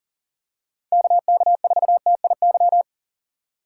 Morse code, heard as a keyed tone, about 40 wpm, 700 Hz. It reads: KK4TIY
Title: USA Call Signs - 6 Characters - 4x Character Spacing - 40wpm